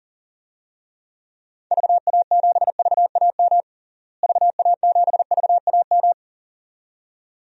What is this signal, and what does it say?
Morse code, heard as a keyed tone, about 40 wpm, 700 Hz. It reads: VA7VAM VA7VAM
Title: DXCC-100 Call Signs - 6 Characters - Repeated 2x - 40wpm